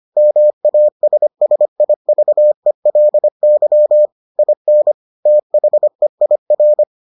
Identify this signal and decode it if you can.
Morse code, heard as a keyed tone, about 25 wpm, 600 Hz. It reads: MASSIVELY IN THEIR